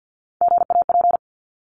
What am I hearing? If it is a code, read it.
Morse code, heard as a keyed tone, about 50 wpm, 700 Hz. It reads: GAP
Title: Top 1501-1750 Words 50wpm